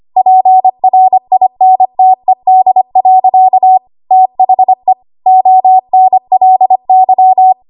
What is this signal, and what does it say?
Morse code, heard as a keyed tone, about 25 wpm, 750 Hz. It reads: PRINTED. THE ONLY
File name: Archive / 25 WPM / 25WPM